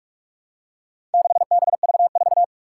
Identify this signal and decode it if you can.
Morse code, heard as a keyed tone, about 46 wpm, 700 Hz. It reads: 6BV4
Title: Four Challenging Letter-Number Combinations 45wpm